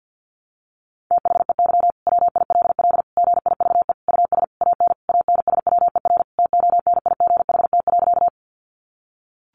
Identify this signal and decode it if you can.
Morse code, heard as a keyed tone, about 49 wpm, 700 Hz. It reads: THEY WILL GIVE US AN ANSWER TONIGHT.
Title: Sentences from Top 600 Words 50wpm